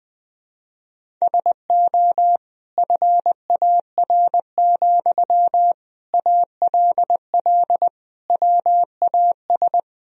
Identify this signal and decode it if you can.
Morse code, heard as a keyed tone, about 20 wpm, 700 Hz. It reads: SO FAR, ALL WAS
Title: Sentences from Top 1000 Words - Review Part 4 - 20wpm